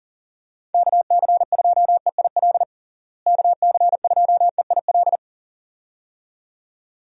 Morse code, heard as a keyed tone, about 40 wpm, 700 Hz. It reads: KC2EIL KC2EIL
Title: USA Call Signs - 6 Characters - Repeated 2x - 40wpm